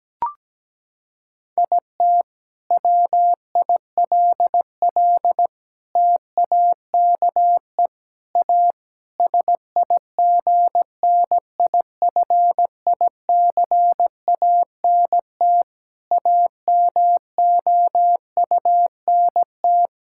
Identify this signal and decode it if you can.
Morse code, heard as a keyed tone, about 17 wpm, 700 Hz. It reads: IT WILL TAKE A SIGNIFICANT AMOUNT
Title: Sentences from Top 1000 Words - Review Part 6 - 17wpm